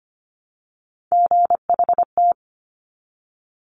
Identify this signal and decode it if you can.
Morse code, heard as a keyed tone, about 25 wpm, 700 Hz. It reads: GHT
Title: Top Three Letter Combinations in Words 25wpm